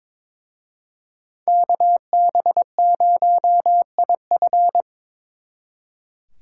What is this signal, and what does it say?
Morse code, heard as a keyed tone, about 25 wpm, 700 Hz. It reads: KB0IF
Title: Speed Racing - USA 2x2 Call Signs 22wpm